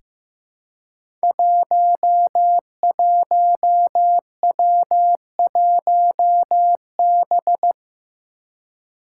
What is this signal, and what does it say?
Morse code, heard as a keyed tone, about 15 wpm, 700 Hz. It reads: 11W1B
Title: Five Challenging Letter-Number Combinations 15wpm